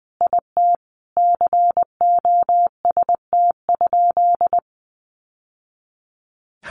Morse code, heard as a keyed tone, about 20 wpm, 700 Hz. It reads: IT COST?
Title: Sentences from Top 800 Words 20wpm